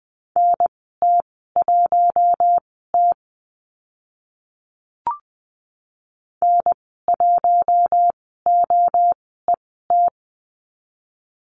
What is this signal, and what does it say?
Morse code, heard as a keyed tone, about 20 wpm, spaced as for 14 wpm, 700 Hz. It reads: NT1T N1OET
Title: 048 Lesson 9g - Number 4 - Call Signs 20wpm